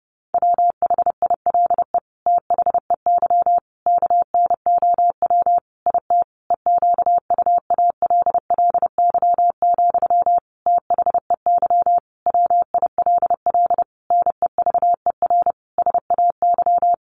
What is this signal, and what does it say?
Morse code, heard as a keyed tone, about 28 wpm, 700 Hz. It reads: WHILE THEY KNOW IT EQUALLY, THEY WILL NEVER SAY
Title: Sentences from Top 1400 Words 30wpm